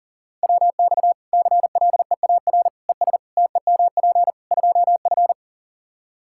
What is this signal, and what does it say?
Morse code, heard as a keyed tone, about 40 wpm, 700 Hz. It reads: WX CLEAR ES TEMP 2F
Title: Practice QSO - Exchange 2 - Info - 40wpm